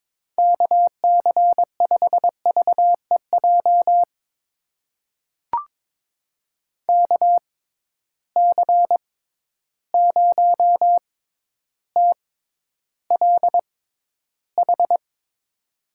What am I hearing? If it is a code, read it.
Morse code, heard as a keyed tone, about 22 wpm, spaced as for 11 wpm, 700 Hz. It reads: KC5VEJ KC0TLH
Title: USA Call Signs - 6 Characters - 6x Character Spacing - 22wpm